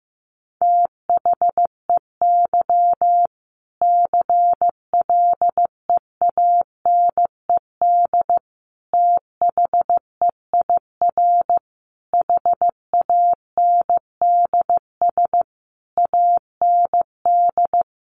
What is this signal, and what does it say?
Morse code, heard as a keyed tone, about 15 wpm, 700 Hz. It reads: THEY CLEANED THEIR HANDS AND
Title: A Sound Of Thunder 15wpm